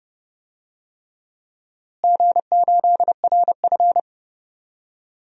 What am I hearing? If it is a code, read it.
Morse code, heard as a keyed tone, about 31 wpm, 700 Hz. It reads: G8RF
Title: DXCC-100 Call Signs - 4 Characters - Repeated 3x - 30wpm